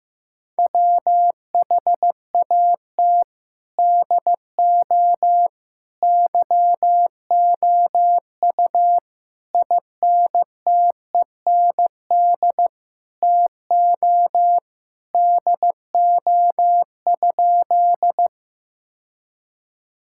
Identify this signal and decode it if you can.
Morse code, heard as a keyed tone, about 15 wpm, 700 Hz. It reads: WHAT DO YOU INTEND TO DO?
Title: Sentences from Top 1600 Words 15wpm